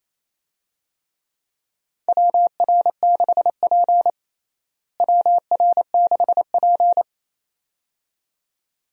Morse code, heard as a keyed tone, about 28 wpm, 700 Hz. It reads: WR6P WR6P
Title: USA Call Signs - 4 Characters - Repeated 2x - 28wpm